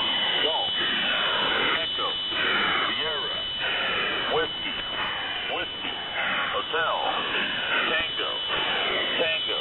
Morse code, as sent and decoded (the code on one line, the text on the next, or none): none